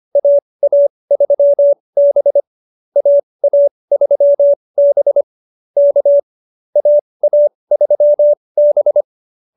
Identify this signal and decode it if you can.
Morse code, heard as a keyed tone, about 25 wpm, spaced as for 20 wpm, 590 Hz. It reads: AA3B AA3B K AA3B
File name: QSO302 20